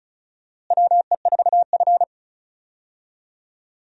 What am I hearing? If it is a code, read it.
Morse code, heard as a keyed tone, about 36 wpm, 700 Hz. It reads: WE4F
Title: USA Call Signs - 4 Characters - Repeated 2x - 35wpm